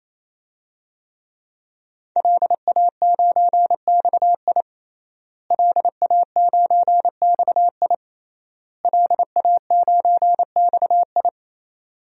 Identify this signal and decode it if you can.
Morse code, heard as a keyed tone, about 28 wpm, 700 Hz. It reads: LA9XI LA9XI LA9XI
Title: DXCC-100 Call Signs - 5 Characters - Repeated 3x - 28wpm